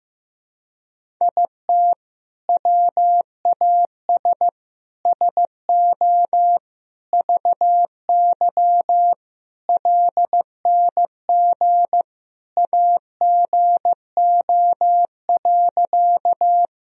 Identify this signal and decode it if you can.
Morse code, heard as a keyed tone, about 15 wpm, 700 Hz. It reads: IT WAS SO VY LNG AGO.
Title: Sentences from Top 500 Words and Abbreviations 15wpm